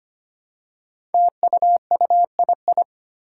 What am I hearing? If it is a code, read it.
Morse code, heard as a keyed tone, about 25 wpm, 700 Hz. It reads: TUUII
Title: Five Challenging Letter-Number Combinations 25wpm